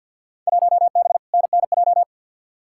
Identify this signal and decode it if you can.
Morse code, heard as a keyed tone, about 48 wpm, 700 Hz. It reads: 1B NNJ
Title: Field Day Exchanges - 50wpm